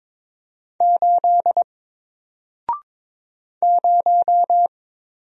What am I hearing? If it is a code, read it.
Morse code, heard as a keyed tone, about 22 wpm, 700 Hz. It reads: 8 0
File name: Single-Letter-Number-22wpm